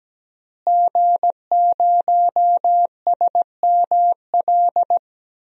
Morse code, heard as a keyed tone, about 17 wpm, 700 Hz. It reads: G0SML
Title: DXCC-100 Call Signs - 5 Characters - Repeated 2x - 17wpm